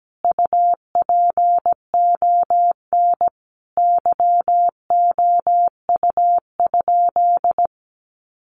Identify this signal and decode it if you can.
Morse code, heard as a keyed tone, about 17 wpm, 700 Hz. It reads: UPON YOU?
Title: Sentences from Top 1100 Words 17wpm